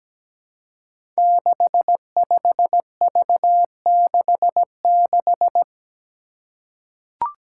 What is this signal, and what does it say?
Morse code, heard as a keyed tone, about 17 wpm, 700 Hz. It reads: 65V66
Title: Five Challenging Letter-Number Combinations 17wpm